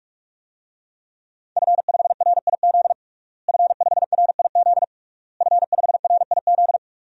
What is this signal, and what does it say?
Morse code, heard as a keyed tone, about 41 wpm, 700 Hz. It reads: F5RIZ F5RIZ F5RIZ
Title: DXCC-100 Call Signs - 5 Characters - Repeated 3x - 45wpm